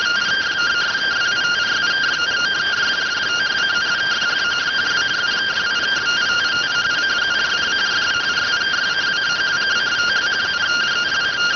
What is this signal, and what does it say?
Two-tone teleprinter signal, about 170 Hz shift.